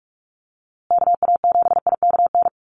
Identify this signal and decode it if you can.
Morse code, heard as a keyed tone, about 44 wpm, 700 Hz. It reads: KA7IKN
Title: USA Call Signs Repeated 2x 45wpm